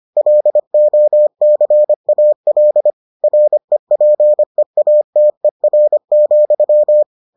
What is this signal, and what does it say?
Morse code, heard as a keyed tone, about 25 wpm, 600 Hz. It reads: LOCAL REPEATER,